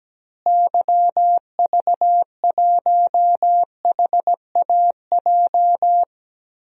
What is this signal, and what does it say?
Morse code, heard as a keyed tone, about 17 wpm, 700 Hz. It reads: YV1HAJ
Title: DXCC-100 Call Signs - 6 Characters - Repeated 2x - 17wpm